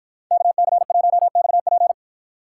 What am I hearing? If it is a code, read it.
Morse code, heard as a keyed tone, about 49 wpm, 700 Hz. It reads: KC1XP